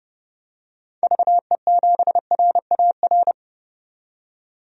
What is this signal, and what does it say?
Morse code, heard as a keyed tone, about 30 wpm, 700 Hz. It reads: VE7RAR
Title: DXCC-100 Call Signs - 6 Characters - 30wpm